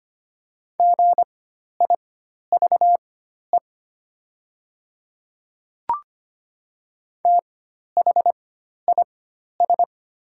Morse code, heard as a keyed tone, about 25 wpm, spaced as for 12 wpm, 700 Hz. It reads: GIVE THIS
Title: 4x Character Spacing - Top 200 Words - 25wpm